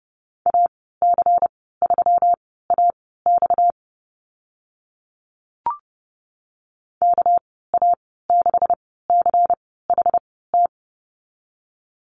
Morse code, heard as a keyed tone, about 30 wpm, spaced as for 17 wpm, 700 Hz. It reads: AC3AX KA6CHT
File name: Lesson 37g - Letter x - Call Signs 30wpm